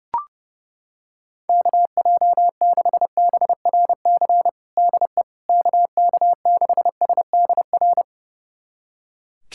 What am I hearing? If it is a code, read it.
Morse code, heard as a keyed tone, about 32 wpm, 700 Hz. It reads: KJ6BRC DE KK6SDR
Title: USA Callsign DE Callsign - 6 Characters - 30wpm